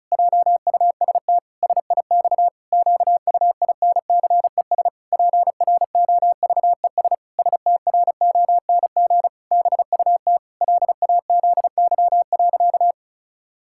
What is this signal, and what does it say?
Morse code, heard as a keyed tone, about 36 wpm, 700 Hz. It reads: JUST SIX QUINCES PROVES STRONG BUT LAZY.
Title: Pangram Challenge v2 - 35wpm